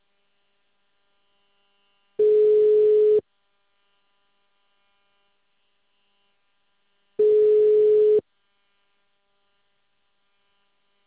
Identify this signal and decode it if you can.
Ringback tone (Europe, 425 Hz).